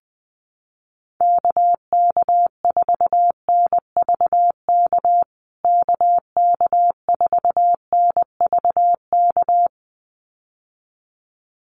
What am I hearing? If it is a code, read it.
Morse code, heard as a keyed tone, about 20 wpm, 700 Hz. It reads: KK4NVK KK4NVK
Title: USA Call Signs Repeated 2x 20wpm